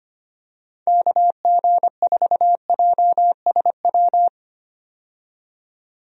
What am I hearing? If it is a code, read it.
Morse code, heard as a keyed tone, about 25 wpm, 700 Hz. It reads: KG4JSW
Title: USA Call Signs - 6 Characters - 25wpm